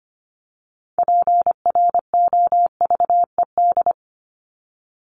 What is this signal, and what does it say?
Morse code, heard as a keyed tone, about 25 wpm, 700 Hz. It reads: PROVED
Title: ED Suffix - Top 100 Words with Base Words 25wpm